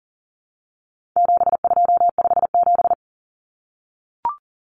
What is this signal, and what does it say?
Morse code, heard as a keyed tone, about 39 wpm, 700 Hz. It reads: 7257
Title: Numbers - 4-Digit 40wpm